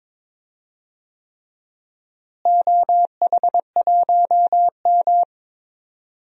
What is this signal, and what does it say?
Morse code, heard as a keyed tone, about 22 wpm, 700 Hz. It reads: OH1M
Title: DXCC-100 Call Signs - 4 Characters - Repeated 3x - 22wpm